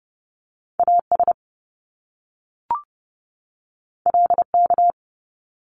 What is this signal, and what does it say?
Morse code, heard as a keyed tone, about 30 wpm, 700 Hz. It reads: AS LK
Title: Top Two Letter Combinations in Words 30wpm